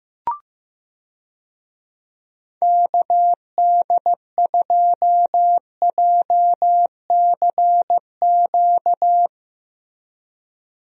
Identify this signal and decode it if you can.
Morse code, heard as a keyed tone, about 15 wpm, 700 Hz. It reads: KD2JCQ
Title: USA Call Signs - 6 Characters - Repeated 3x - 15wpm